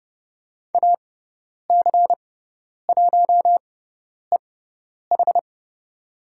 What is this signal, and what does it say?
Morse code, heard as a keyed tone, about 30 wpm, spaced as for 10 wpm, 700 Hz. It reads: AC1EH
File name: Lesson 18g - Letter m - Call Signs 30wpm